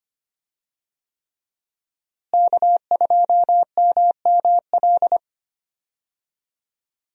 Morse code, heard as a keyed tone, about 25 wpm, 700 Hz. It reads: K2MML
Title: USA Call Signs - 5 Characters - Repeated 3x - 25wpm